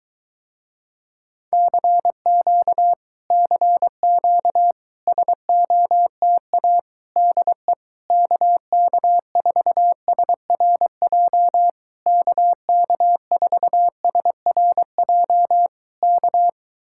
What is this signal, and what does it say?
Morse code, heard as a keyed tone, about 23 wpm, 700 Hz. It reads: CQ CQ SOTA DE KK4SRJ KK4SRJ K